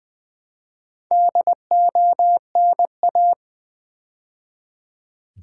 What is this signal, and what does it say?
Morse code, heard as a keyed tone, about 20 wpm, 700 Hz. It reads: DONA
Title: Build A Word - 2 Letters At A Time - 6-Letter Words - 20wpm